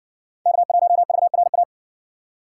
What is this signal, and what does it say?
Morse code, heard as a keyed tone, about 57 wpm, 700 Hz. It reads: K1VRA